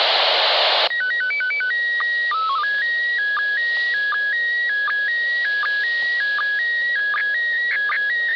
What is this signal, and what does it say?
SSTV (slow-scan television), PD 180.
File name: SSTV / ISS2015AprilPD180x